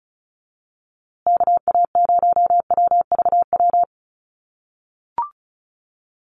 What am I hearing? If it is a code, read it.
Morse code, heard as a keyed tone, about 34 wpm, 700 Hz. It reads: KA0WVW
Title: USA Call Signs Repeated 3x 35wpm